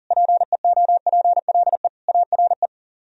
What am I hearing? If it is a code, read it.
Morse code, heard as a keyed tone, about 55 wpm, spaced as for 47 wpm, 700 Hz. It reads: PEOPLE ARE
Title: Speed-Racing - Sets of 5 Words 40wpm